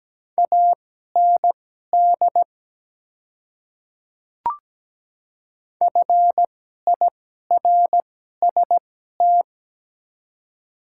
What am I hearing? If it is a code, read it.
Morse code, heard as a keyed tone, about 17 wpm, spaced as for 12 wpm, 690 Hz. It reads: AND FIRST
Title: 2x Character Spacing - Top 300 Words - 17wpm